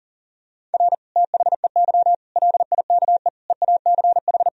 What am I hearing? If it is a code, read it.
Morse code, heard as a keyed tone, about 40 wpm, 700 Hz. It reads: R THEY LIKE EACH
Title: Sentences from Top 200 Words and Abbreviations 40wpm